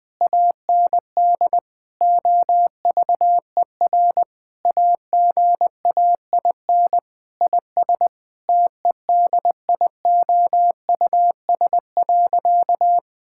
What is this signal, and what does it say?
Morse code, heard as a keyed tone, about 20 wpm, 700 Hz. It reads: AND OVER AGAIN IS TEDIOUS.
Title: Sentences with 7-Word Phrases - 20wpm